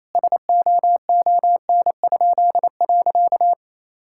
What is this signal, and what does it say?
Morse code, heard as a keyed tone, about 28 wpm, 700 Hz. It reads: SOON?.
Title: Pangram Challenge v2 - 28wpm